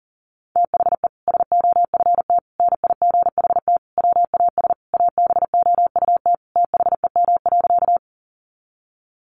Morse code, heard as a keyed tone, about 37 wpm, 700 Hz. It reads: THE SOFT NIGHT WAS ABOUT THEM.
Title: Sentences from Top 1500 Words 40wpm